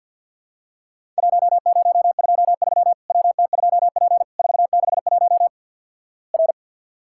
Morse code, heard as a keyed tone, about 50 wpm, 700 Hz. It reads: 1023 WT2P 461 R
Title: Worked All Europe DX Contest - QTC Exchange - 50wpm